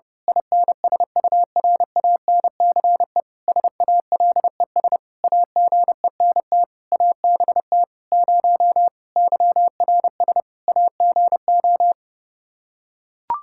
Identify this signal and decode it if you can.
Morse code, heard as a keyed tone, about 31 wpm, 700 Hz. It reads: INSURANCE SALES AGENT ABT 0 YRS AGO
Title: Practice QSO - Exchange 3 - More Info - 30wpm